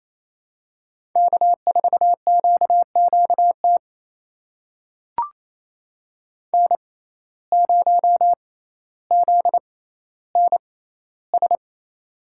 Morse code, heard as a keyed tone, about 28 wpm, spaced as for 14 wpm, 700 Hz. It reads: K4QQT N0ZNS
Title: USA Call Signs - 5 Characters - 6x Character Spacing - 28wpm